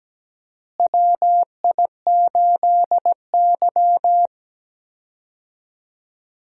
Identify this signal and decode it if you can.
Morse code, heard as a keyed tone, about 17 wpm, 700 Hz. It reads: WI8Y
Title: USA Call Signs - 4 Characters - Repeated 3x - 17wpm